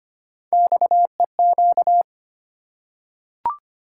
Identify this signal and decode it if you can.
Morse code, heard as a keyed tone, about 25 wpm, 700 Hz. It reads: XEQ
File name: Three Letter-Number Combinations - 25wpm